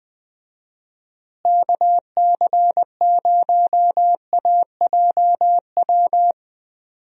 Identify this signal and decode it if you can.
Morse code, heard as a keyed tone, about 20 wpm, 700 Hz. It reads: KC0AJW
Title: USA Call Signs - 6 Characters - Repeated 3x - 20wpm